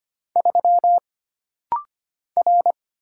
Morse code, heard as a keyed tone, about 25 wpm, 700 Hz. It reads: 3 R
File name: Single Character (Letter:Number:Punctuation:Prosign) - Rapid-Fire - 25wpm